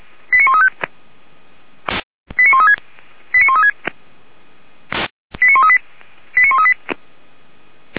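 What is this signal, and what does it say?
Five-tone selective call (ZVEI), heard as quick runs of five tones, tones of about 70 ms. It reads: 89135 89136 89136 89138 89138
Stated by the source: ZVEI five-tone selective call 89134 received by SDR.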